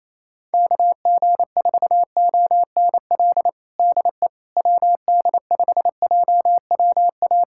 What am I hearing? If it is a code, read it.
Morse code, heard as a keyed tone, about 28 wpm, 700 Hz. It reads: KG4ONL DE WD5JWA
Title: USA Callsign DE Callsign - 6 Characters - 28wpm